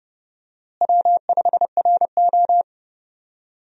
Morse code, heard as a keyed tone, about 30 wpm, 700 Hz. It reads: W5RO
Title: USA Call Signs - 4 Characters - Repeated 2x - 30wpm